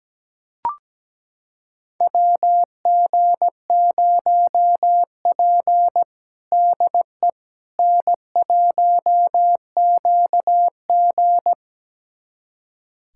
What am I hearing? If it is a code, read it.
Morse code, heard as a keyed tone, about 17 wpm, 700 Hz. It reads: WG0P DE N1QG
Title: USA Callsign DE Callsign - 4 Characters - 17wpm